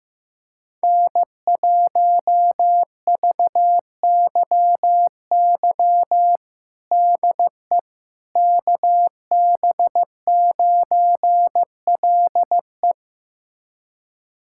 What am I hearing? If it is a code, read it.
Morse code, heard as a keyed tone, about 15 wpm, 700 Hz. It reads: N1VYY DE KB9LE
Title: USA Callsign DE Callsign - 5 Characters - 15wpm